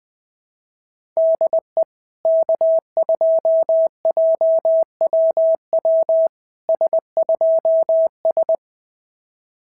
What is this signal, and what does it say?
Morse code, heard as a keyed tone, about 20 wpm, 690 Hz. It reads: DE K2JWW S2S
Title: SOTA QSO Practice 20wpm